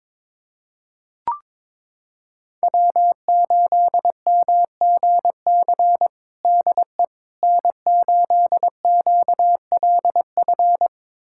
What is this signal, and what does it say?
Morse code, heard as a keyed tone, about 22 wpm, 700 Hz. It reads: W8MGC DE N8QLF
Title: USA Callsign DE Callsign - 5 Characters - 22wpm